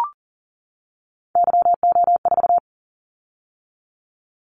Morse code, heard as a keyed tone, about 37 wpm, 700 Hz. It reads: YO4
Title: Callsign Prefixes for Top 25 DXCC - 40wpm